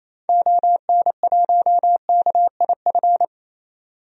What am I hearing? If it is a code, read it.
Morse code, heard as a keyed tone, about 28 wpm, 700 Hz. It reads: ON1KIF
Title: DXCC-100 Call Signs - 6 Characters - 28wpm